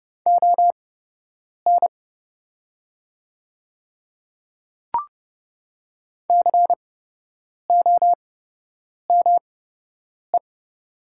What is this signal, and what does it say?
Morse code, heard as a keyed tone, about 29 wpm, spaced as for 8 wpm, 700 Hz. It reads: ON COME
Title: 8x Character Spacing - Top 100 Words - 30wpm